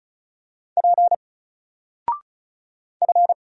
Morse code, heard as a keyed tone, about 36 wpm, 700 Hz. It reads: P F
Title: Single Character (Letter:Number:Punctuation:Prosign) Alt 35wpm